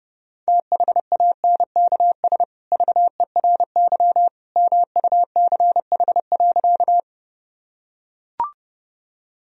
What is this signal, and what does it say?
Morse code, heard as a keyed tone, about 30 wpm, 700 Hz. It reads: THANKS VERY MUCH.
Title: Sentences from Top 1000 Words - Review Part 3 - 30wpm